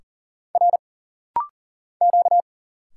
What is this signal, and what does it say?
Morse code, heard as a keyed tone, about 40 wpm, 700 Hz. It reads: R Q
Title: Single Letters - Rapid-Fire 40wpm